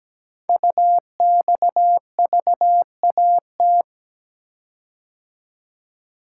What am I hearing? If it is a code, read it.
Morse code, heard as a keyed tone, about 17 wpm, 700 Hz. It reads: UXVAT
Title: Code Groups - 5 Letters - 17wpm